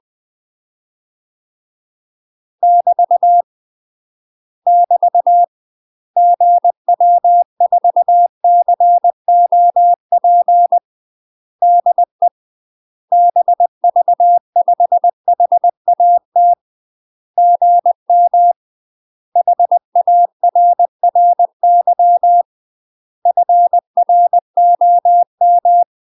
Morse code, heard as a keyed tone, about 20 wpm, 700 Hz. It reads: <BT> <BT> GW4COP DE BV5HAT GM HARRY FROM
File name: UK MASTER QSOs TRAINING / 20WPM / UK MASTER QSOs 0012